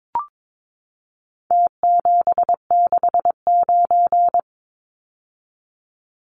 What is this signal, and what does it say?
Morse code, heard as a keyed tone, about 22 wpm, 700 Hz. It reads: T769
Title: Numbers - Contest Style - 4-Digit 22wpm